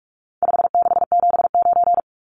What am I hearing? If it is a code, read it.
Morse code, heard as a keyed tone, about 44 wpm, 700 Hz. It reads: 5679
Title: Numbers - 4-Digit 45wpm